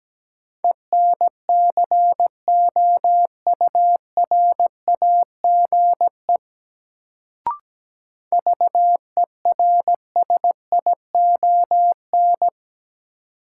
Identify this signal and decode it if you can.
Morse code, heard as a keyed tone, about 17 wpm, 700 Hz. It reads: ENCOURAGE VERSION
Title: Sentences from Top 1400 Words - New Word Drill - 17wpm